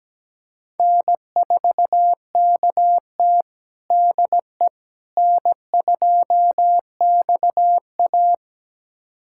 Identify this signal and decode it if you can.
Morse code, heard as a keyed tone, about 17 wpm, 700 Hz. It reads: N4KT DE N2XA
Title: USA Callsign DE Callsign - 4 Characters - 17wpm